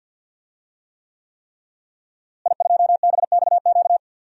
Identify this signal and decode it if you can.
Morse code, heard as a keyed tone, about 49 wpm, 700 Hz. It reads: I2BXQ
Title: DXCC-100 Call Signs - 5 Characters - Repeated 3x - 50wpm